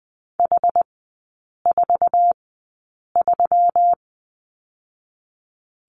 Morse code, heard as a keyed tone, about 20 wpm, spaced as for 8 wpm, 700 Hz. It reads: H43
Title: 121 Lesson 20e - Number 3 - Three Characters 20wpm